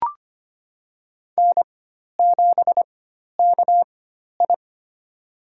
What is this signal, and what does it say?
Morse code, heard as a keyed tone, about 25 wpm, spaced as for 12 wpm, 700 Hz. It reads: N7KI
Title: USA Call Signs - 4 Characters - 4x Character Spacing - 25wpm